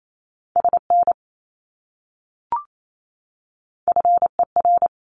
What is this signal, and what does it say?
Morse code, heard as a keyed tone, about 28 wpm, 700 Hz. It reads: SN FER
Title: Abbreviations Repeated 3x - 28wpm